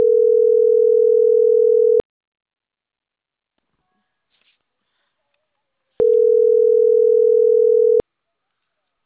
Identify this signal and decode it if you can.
Ringback tone (North America), 440 + 480 Hz.
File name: CENTRAL CAEA / out agcaea02